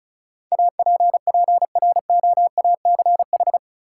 Morse code, heard as a keyed tone, about 36 wpm, 700 Hz. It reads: APPROACH
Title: Sentences from Top 1400 Words - New Word Drill Alt - 35wpm